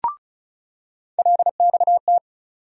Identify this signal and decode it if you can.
Morse code, heard as a keyed tone, about 36 wpm, 700 Hz. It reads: LXT
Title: USA Call Sign Postfixes - 3 Letters 35wpm